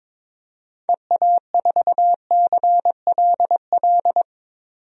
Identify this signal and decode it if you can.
Morse code, heard as a keyed tone, about 22 wpm, 700 Hz. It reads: EA4CLL
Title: DXCC-100 Call Signs - 6 Characters - 22wpm